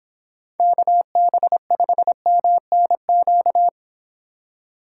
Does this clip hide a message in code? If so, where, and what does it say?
Yes: Morse code from 0.6 to 3.7 s, heard as a keyed tone, about 25 wpm, 700 Hz. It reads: KB5MNQ